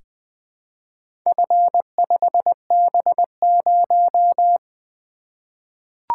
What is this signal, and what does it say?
Morse code, heard as a keyed tone, about 20 wpm, 700 Hz. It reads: F5B0
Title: Four Letter-Number Combinations - 20wpm